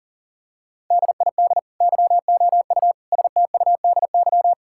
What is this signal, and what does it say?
Morse code, heard as a keyed tone, about 40 wpm, 700 Hz. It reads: DID YOU STUDY
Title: Sentences from Top 1000 Words - Review Part 6 - 40wpm